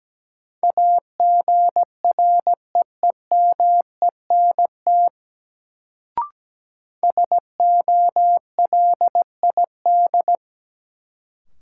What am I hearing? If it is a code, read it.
Morse code, heard as a keyed tone, about 17 wpm, 700 Hz. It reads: AGREEMENT SOLID
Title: Sentences from Top 1500 Words - New Word Drill - 17wpm